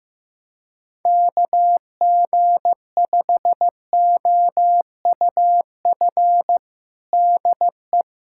Morse code, heard as a keyed tone, about 15 wpm, 700 Hz. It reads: KG5OUF DE
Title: USA Callsign DE Callsign 15wpm